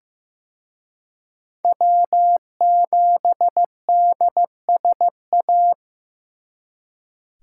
Morse code, heard as a keyed tone, about 18 wpm, 700 Hz. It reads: W7DSA
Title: Speed Racing - USA 1x3 Call Signs 15wpm